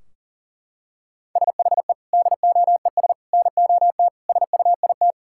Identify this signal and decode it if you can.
Morse code, heard as a keyed tone, about 40 wpm, 700 Hz. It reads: SHE DOES NOT SUIT
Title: Sentences from Top 1600 Words 40wpm